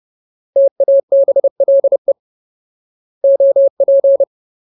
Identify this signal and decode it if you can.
Morse code, heard as a keyed tone, about 30 wpm, 550 Hz. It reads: TABLE OP
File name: Ultimate Word List / 30WPM Ultimate